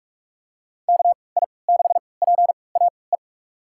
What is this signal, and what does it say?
Morse code, heard as a keyed tone, about 46 wpm, spaced as for 25 wpm, 700 Hz. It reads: KI6PAE
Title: USA Call Signs - 6 Characters - 3x Character Spacing - 45wpm